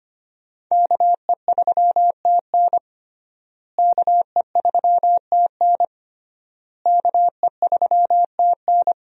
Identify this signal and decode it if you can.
Morse code, heard as a keyed tone, about 25 wpm, 700 Hz. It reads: KE3TN KE3TN KE3TN
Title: USA Call Signs - 5 Characters - Repeated 3x - 25wpm